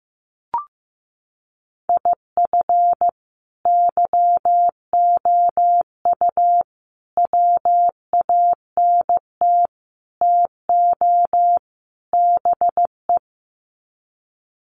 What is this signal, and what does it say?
Morse code, heard as a keyed tone, about 15 wpm, 700 Hz. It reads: IF YOU WANT TO BE
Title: Sets of 5 Words Encore 15wpm